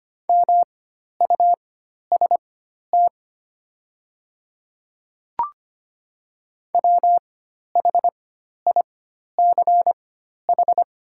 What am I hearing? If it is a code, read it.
Morse code, heard as a keyed tone, about 25 wpm, spaced as for 12 wpm, 700 Hz. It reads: MUST WHICH
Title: 4x Character Spacing - Top 200 Words - 25wpm